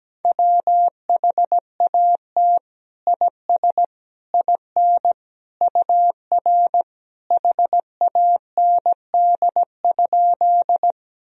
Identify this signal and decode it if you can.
Morse code, heard as a keyed tone, about 17 wpm, 700 Hz. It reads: WHAT IS IN UR HAND?
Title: Sentences from Top 200 Words and Abbreviations 17wpm